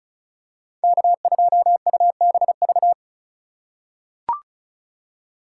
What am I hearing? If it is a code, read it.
Morse code, heard as a keyed tone, about 36 wpm, 700 Hz. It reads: K2UBV
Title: USA Call Signs - 5 Characters - 3x Character Spacing - 35wpm